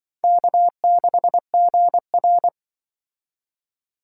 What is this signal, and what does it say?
Morse code, heard as a keyed tone, about 24 wpm, 700 Hz. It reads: K6GR
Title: Speed Racing - USA 1x2 Call Signs 20wpm